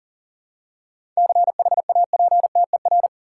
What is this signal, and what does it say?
Morse code, heard as a keyed tone, about 40 wpm, 700 Hz. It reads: CHAPTER
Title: Seven Letter Words - 40wpm